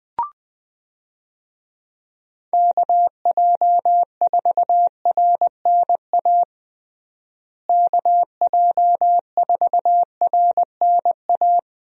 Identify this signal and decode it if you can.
Morse code, heard as a keyed tone, about 21 wpm, 700 Hz. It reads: KJ4RNA KJ4RNA
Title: USA Call Signs - 6 Characters - Repeated 2x - 20wpm